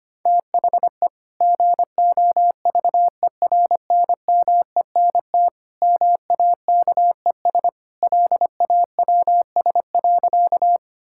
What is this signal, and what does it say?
Morse code, heard as a keyed tone, about 25 wpm, 700 Hz. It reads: THE GOVERNMENT MAKES LAWS.
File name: practice / 25wpm / Sentences with 2-Word Phrases - 25wpm